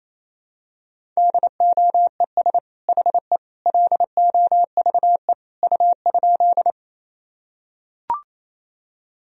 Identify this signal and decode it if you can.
Morse code, heard as a keyed tone, about 28 wpm, 700 Hz. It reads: DOES HE LOVE U?
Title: Sentences from Top 400 Words and Abbreviations 28wpm